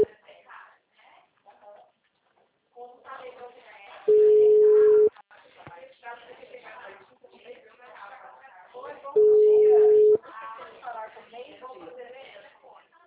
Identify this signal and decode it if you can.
Ringback tone (Europe, 425 Hz), with speech.